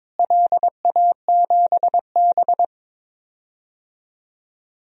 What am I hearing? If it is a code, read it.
Morse code, heard as a keyed tone, about 22 wpm, 700 Hz. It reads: LA7B
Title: DXCC-100 Call Signs - 4 Characters - Repeated 2x - 22wpm